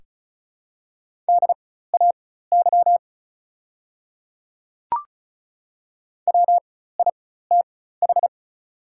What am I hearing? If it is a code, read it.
Morse code, heard as a keyed tone, about 33 wpm, spaced as for 16 wpm, 690 Hz. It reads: DAY WITH
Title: 4x Character Spacing - Top 100 Words - 35wpm